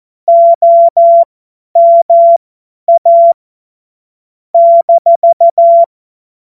Morse code, heard as a keyed tone, about 14 wpm, spaced as for 10 wpm, 680 Hz. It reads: OMA -
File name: Morsekurs / Witze / Witz 19. Oktober 2021, 50 F70